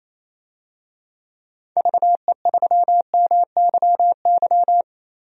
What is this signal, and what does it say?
Morse code, heard as a keyed tone, about 28 wpm, 700 Hz. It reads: VE3MYY
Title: DXCC-100 Call Signs - 6 Characters - 28wpm